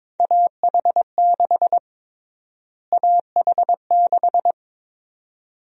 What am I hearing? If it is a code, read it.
Morse code, heard as a keyed tone, about 22 wpm, 700 Hz. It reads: AH6 AH6
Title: USA Call Sign Prefixes - Repeated 2x - 22wpm